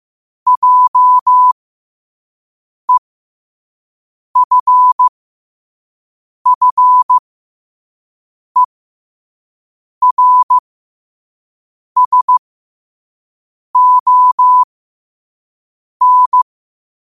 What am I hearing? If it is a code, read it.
Morse code, heard as a keyed tone, about 15 wpm, spaced as for 5 wpm, 1000 Hz. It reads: JEFFERSON
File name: 5wpm